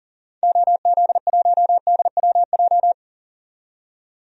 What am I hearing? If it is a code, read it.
Morse code, heard as a keyed tone, about 40 wpm, 700 Hz. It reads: OZ1DWJ
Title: DXCC-100 Call Signs - 6 Characters - 40wpm